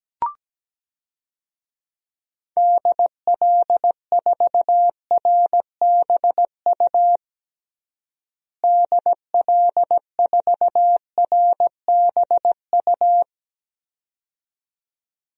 Morse code, heard as a keyed tone, about 17 wpm, 700 Hz. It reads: DL4RBU DL4RBU
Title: DXCC-100 Call Signs - 6 Characters - Repeated 2x - 17wpm